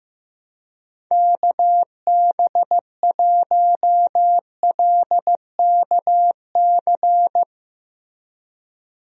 Morse code, heard as a keyed tone, about 15 wpm, 700 Hz. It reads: KB1LKC
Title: USA Call Signs - 6 Characters - Repeated 2x - 15wpm